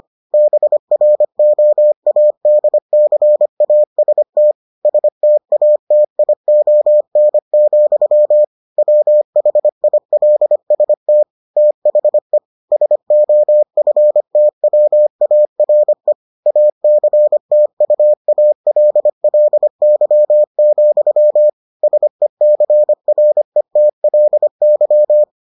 Morse code, heard as a keyed tone, about 25 wpm, 600 Hz. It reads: BROADCAST STATION, WHILST THE SOFTWARE ACTUALLY, SECRETLY